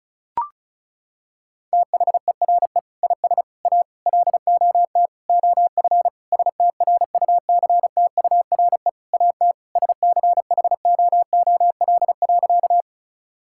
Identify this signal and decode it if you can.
Morse code, heard as a keyed tone, about 36 wpm, 700 Hz. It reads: THERE IS A LOT OF STRUCTURE AT SCHOOL.
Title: Sentences from Top 1000 Words - Review Part 6 - 35wpm